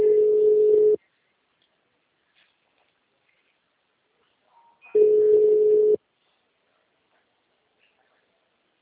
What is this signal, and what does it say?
Ringback tone (Europe, 425 Hz).
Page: Seu Credito